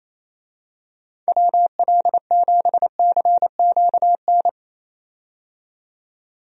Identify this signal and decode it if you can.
Morse code, heard as a keyed tone, about 28 wpm, 700 Hz. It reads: WL7CQN
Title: USA Call Signs - 6 Characters - 28wpm